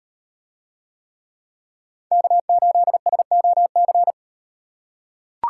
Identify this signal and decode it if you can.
Morse code, heard as a keyed tone, about 38 wpm, 700 Hz. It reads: K8SOC